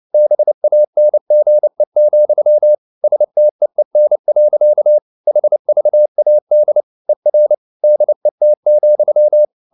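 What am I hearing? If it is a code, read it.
Morse code, heard as a keyed tone, about 29 wpm, 600 Hz. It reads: BANGE, STEEN. HVAD ER DET,
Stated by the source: Fast (QRQ) Morse at 29 wpm, from CW med Gnister.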